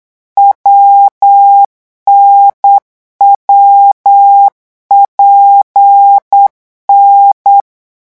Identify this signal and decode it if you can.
Morse code, heard as a keyed tone, about 8 wpm, 800 Hz. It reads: WNWPN
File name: morse mp3 / revision8-vitesse 8 mots